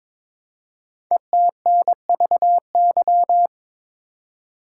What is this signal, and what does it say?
Morse code, heard as a keyed tone, about 22 wpm, 700 Hz. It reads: ETNVY
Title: Code Groups - 5 Letters - 22wpm